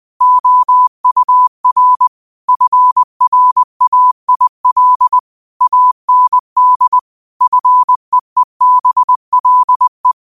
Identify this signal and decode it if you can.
Morse code, heard as a keyed tone, about 20 wpm, 1000 Hz. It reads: OUR FRAIL AND FEEBLE
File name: qotdfiles / 20wpm